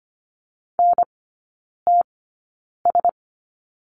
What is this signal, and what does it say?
Morse code, heard as a keyed tone, about 25 wpm, spaced as for 9 wpm, 700 Hz. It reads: NTS
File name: course / 25:10wpm / 093 Lesson 16e - Letter c - Three Characters 25wpm